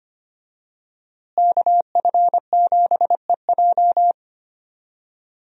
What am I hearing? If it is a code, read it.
Morse code, heard as a keyed tone, about 25 wpm, 700 Hz. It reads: KF7EJ
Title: USA Call Signs - 5 Characters - Repeated 3x - 25wpm